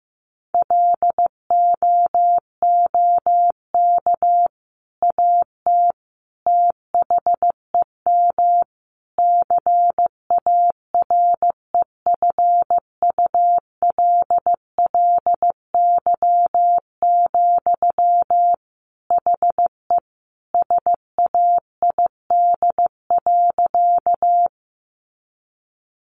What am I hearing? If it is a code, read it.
Morse code, heard as a keyed tone, about 15 wpm, 700 Hz. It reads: LOOK AT THEM CAREFULLY, HE SAID.
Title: Sentences from Top 1300 Words 15wpm